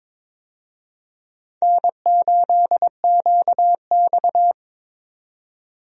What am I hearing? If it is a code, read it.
Morse code, heard as a keyed tone, about 22 wpm, 700 Hz. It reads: N8QX
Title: USA Call Signs - 4 Characters - 22wpm